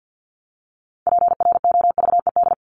Morse code, heard as a keyed tone, about 54 wpm, 700 Hz. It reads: PROVED
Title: ED Suffix - Top 100 Words with Base Words 50wpm